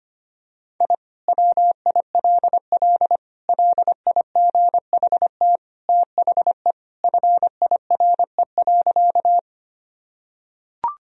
Morse code, heard as a keyed tone, about 25 wpm, 700 Hz. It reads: I WILL LIGHT THE FIRE.
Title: Sentences from Top 1000 Words - Review Part 1 - 25wpm